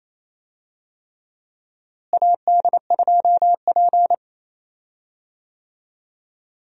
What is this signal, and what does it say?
Morse code, heard as a keyed tone, about 28 wpm, 700 Hz. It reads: AD2P
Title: USA Call Signs - 4 Characters - Repeated 3x - 28wpm